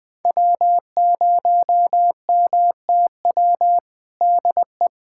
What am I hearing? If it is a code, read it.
Morse code, heard as a keyed tone, about 21 wpm, 700 Hz. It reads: W0MTW DE
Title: USA Callsign DE Callsign - 5 Characters - 20wpm